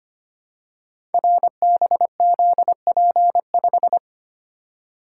Morse code, heard as a keyed tone, about 25 wpm, 700 Hz. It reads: RBZP5
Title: Code Groups - 5 Letter:Numbers - 25wpm